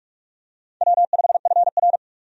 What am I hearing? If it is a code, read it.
Morse code, heard as a keyed tone, about 46 wpm, 700 Hz. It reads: W5FR
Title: USA Call Signs - 4 Characters - 45wpm